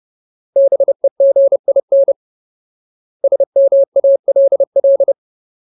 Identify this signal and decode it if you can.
Morse code, heard as a keyed tone, about 30 wpm, 550 Hz. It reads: BEGIN SMALL
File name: Ultimate Word List / 30WPM Ultimate